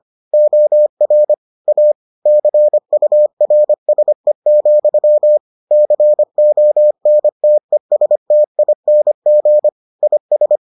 Morse code, heard as a keyed tone, about 25 wpm, 600 Hz. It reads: OR A CURSE, CONTESTING IS